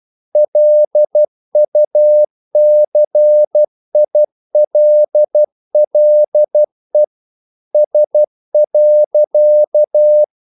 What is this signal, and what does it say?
Morse code, heard as a keyed tone, about 12 wpm, 600 Hz. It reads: LUCILLE S.
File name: CW / Practice / EnglishQuotations0312